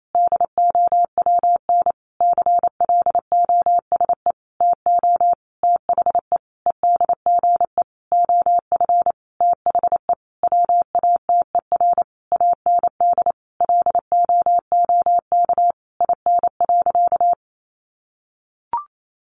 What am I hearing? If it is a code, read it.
Morse code, heard as a keyed tone, about 28 wpm, 700 Hz. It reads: DOWN CLOSE TO THE EDGE OF THE WATER AND LOOK IN.
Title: Sentences from Top 1300 Words 28wpm